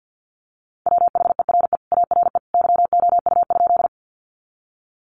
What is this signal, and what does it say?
Morse code, heard as a keyed tone, about 50 wpm, 700 Hz. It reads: WHERE ARE YOU?
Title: Sentences from Top 200 Words 50wpm